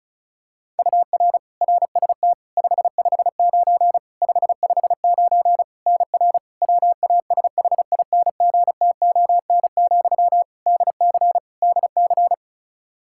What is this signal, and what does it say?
Morse code, heard as a keyed tone, about 36 wpm, 700 Hz. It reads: UR RST 559 559 NR WASHINGTON, DC DC
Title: Practice QSO - Exchange 1 - Beginning - 35wpm